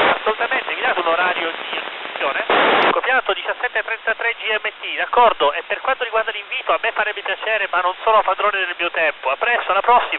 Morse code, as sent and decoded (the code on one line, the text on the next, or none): none